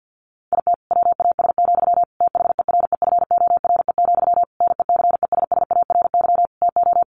Morse code, heard as a keyed tone, about 50 wpm, 700 Hz. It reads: IT WAS, THEREFORE, NECESSARY TO
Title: Sentences from Top 1500 Words 50wpm